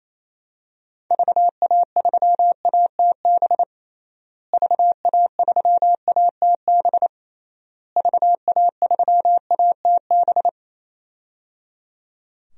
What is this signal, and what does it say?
Morse code, heard as a keyed tone, about 28 wpm, 700 Hz. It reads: VA3ATB VA3ATB VA3ATB
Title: DXCC-100 Call Signs - 6 Characters - Repeated 3x - 28wpm